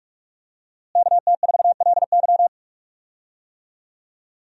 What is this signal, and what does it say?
Morse code, heard as a keyed tone, about 46 wpm, 700 Hz. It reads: KT4LY
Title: USA Call Signs - 5 Characters - Repeated 2x - 45wpm